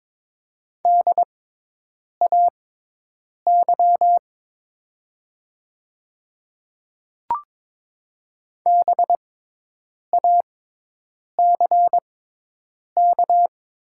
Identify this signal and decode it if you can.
Morse code, heard as a keyed tone, about 22 wpm, spaced as for 8 wpm, 700 Hz. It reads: DAY BACK
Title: 6x Character Spacing - Top 200 Words - 22wpm